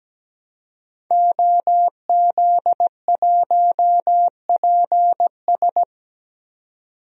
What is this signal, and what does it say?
Morse code, heard as a keyed tone, about 17 wpm, 700 Hz. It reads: OZ1PS
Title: DXCC-100 Call Signs - 5 Characters - 17wpm